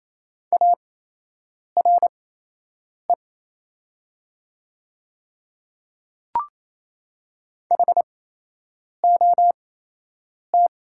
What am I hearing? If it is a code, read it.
Morse code, heard as a keyed tone, about 27 wpm, spaced as for 8 wpm, 700 Hz. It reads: ARE HOT
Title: 8x Character Spacing - Top 100 Words - 28wpm